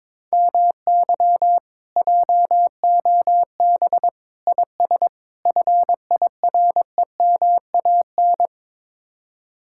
Morse code, heard as a keyed tone, about 23 wpm, 700 Hz. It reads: MY JOB IS FIREMAN
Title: Speed-Racing Common QSO Phrases Encore - 22wpm